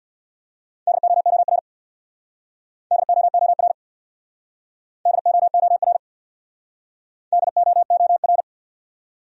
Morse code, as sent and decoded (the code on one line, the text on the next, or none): -.. --- --- .-. / -.. --- --- .-. / -.. --- --- .-. / -.. --- --- .-.
DOOR DOOR DOOR DOOR